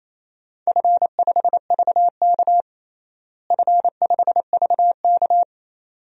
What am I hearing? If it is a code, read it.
Morse code, heard as a keyed tone, about 27 wpm, 700 Hz. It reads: F5VK F5VK
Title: DXCC-100 Call Signs - 4 Characters - Repeated 2x - 28wpm